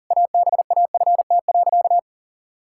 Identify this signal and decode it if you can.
Morse code, heard as a keyed tone, about 37 wpm, 700 Hz. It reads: ABAFT.
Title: Pangram Challenge v2 - 40wpm